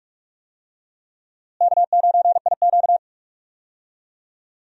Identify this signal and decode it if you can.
Morse code, heard as a keyed tone, about 42 wpm, 700 Hz. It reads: K9IQ
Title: USA Call Signs - 4 Characters - Repeated 2x - 45wpm